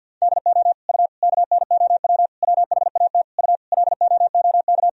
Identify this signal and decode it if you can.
Morse code, heard as a keyed tone, about 47 wpm, 700 Hz. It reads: DO U KNOW WHAT U LOOK
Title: Sentences from Top 100 Words and Abbreviations 50wpm